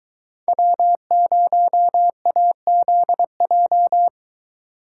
Morse code, heard as a keyed tone, about 22 wpm, 700 Hz. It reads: W0AZJ